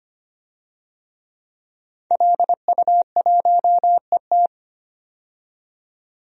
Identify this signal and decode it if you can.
Morse code, heard as a keyed tone, about 25 wpm, 700 Hz. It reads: LU1ET
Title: DXCC-100 Call Signs - 5 Characters - 25wpm